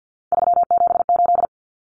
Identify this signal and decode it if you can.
Morse code, heard as a keyed tone, about 50 wpm, 700 Hz. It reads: 378
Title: Common Cut Numbers - 3-Digit 50wpm